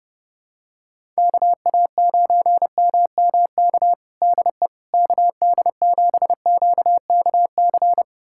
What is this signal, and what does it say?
Morse code, heard as a keyed tone, about 32 wpm, 700 Hz. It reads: KA9MMK DE KD7QKC
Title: USA Callsign DE Callsign 30wpm